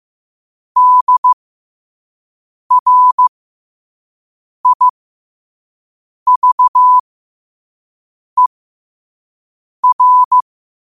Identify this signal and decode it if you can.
Morse code, heard as a keyed tone, about 15 wpm, spaced as for 5 wpm, 1000 Hz. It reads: DRIVER
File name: qotdfiles / 5wpm